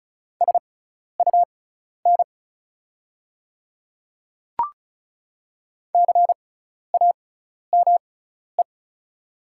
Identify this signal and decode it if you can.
Morse code, heard as a keyed tone, about 34 wpm, spaced as for 12 wpm, 700 Hz. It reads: SUN CAME
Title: 6x Character Spacing - Top 500 Words - 35wpm